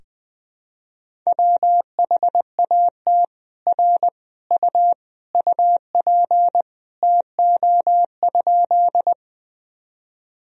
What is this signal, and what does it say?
Morse code, heard as a keyed tone, about 20 wpm, 700 Hz. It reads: WHAT R U UP TO?
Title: Sentences from Top 100 Words and Abbreviations 20wpm